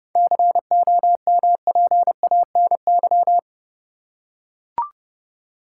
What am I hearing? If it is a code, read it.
Morse code, heard as a keyed tone, about 30 wpm, 700 Hz. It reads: COMPANY
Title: Seven Letter Words - 30wpm